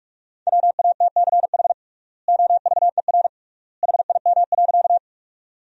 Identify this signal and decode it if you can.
Morse code, heard as a keyed tone, about 46 wpm, 700 Hz. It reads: WATCH OVER HIM.
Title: 3x Word-Spacing - Sentences from Top 300 Words - 45wpm